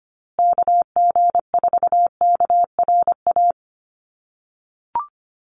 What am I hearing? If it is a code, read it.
Morse code, heard as a keyed tone, about 25 wpm, 700 Hz. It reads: KG4KRA
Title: USA Call Signs Repeated 2x 25wpm